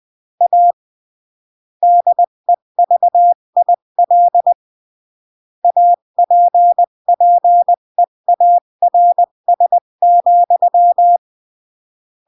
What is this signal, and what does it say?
Morse code, heard as a keyed tone, about 20 wpm, 700 Hz. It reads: A DEVIL APPEARS,
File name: CW / jodorowsky / 82 commandments 20-15